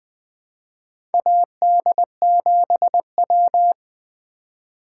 Morse code, heard as a keyed tone, about 21 wpm, 700 Hz. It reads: AD7W
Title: USA Call Signs - 4 Characters - 4x Character Spacing - 20wpm